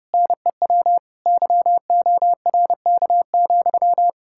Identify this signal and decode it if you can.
Morse code, heard as a keyed tone, about 36 wpm, 700 Hz. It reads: NEW YORK,
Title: Speed-Racing Common QSO Phrases Encore - 30wpm